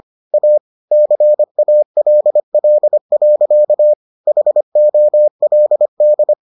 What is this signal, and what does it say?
Morse code, heard as a keyed tone, about 25 wpm, 600 Hz. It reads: A CALL. HOLD